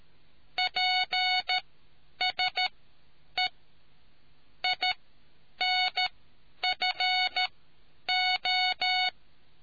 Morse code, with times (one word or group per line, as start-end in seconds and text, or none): 0.6-3.5 PSE
4.6-9.1 INFO